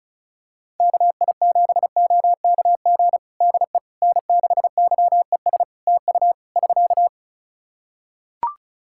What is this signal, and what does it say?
Morse code, heard as a keyed tone, about 36 wpm, 700 Hz. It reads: KI7OKG DE N6YES TU <SK>
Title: Practice QSO - Exchange 4 - Ending - 35wpm